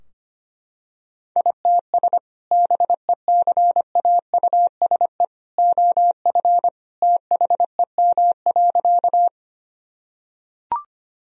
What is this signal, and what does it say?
Morse code, heard as a keyed tone, about 25 wpm, 700 Hz. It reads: ITS BECAUSE OF THEM.
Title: Sentences from Top 1000 Words - Review Part 3 - 25wpm